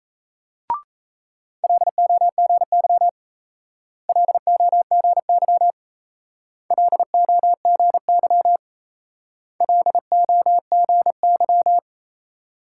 Morse code, heard as a keyed tone, about 35 wpm, 700 Hz. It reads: LOGY LOGY LOGY LOGY